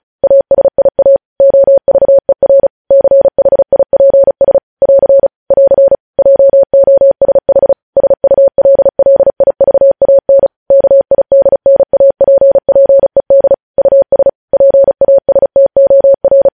Morse code, heard as a keyed tone, about 36 wpm, 550 Hz. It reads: ASIA OVER CHIPS <AR> <AR> JOSH SULLIVAN KIDNAPPED US PASTOR
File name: podcast / News headlines-1x-35x35